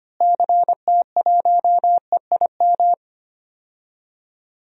Morse code, heard as a keyed tone, about 25 wpm, 700 Hz. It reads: CT1EIM
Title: DXCC-100 Call Signs - 6 Characters - Repeated 3x - 25wpm